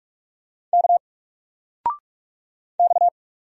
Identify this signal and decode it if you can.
Morse code, heard as a keyed tone, about 44 wpm, 700 Hz. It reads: K X
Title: Single Character (Letter:Number:Punctuation:Prosign) 45wpm